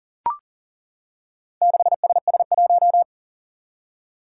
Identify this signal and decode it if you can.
Morse code, heard as a keyed tone, about 40 wpm, 700 Hz. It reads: 6SS1
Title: Four Challenging Letter-Number Combinations 40wpm